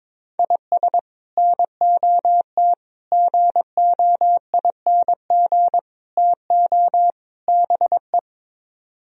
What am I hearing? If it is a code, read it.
Morse code, heard as a keyed tone, about 22 wpm, 700 Hz. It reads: IS NOT GOING TO BE
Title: Speed-Racing - Sets of 5 Words 22wpm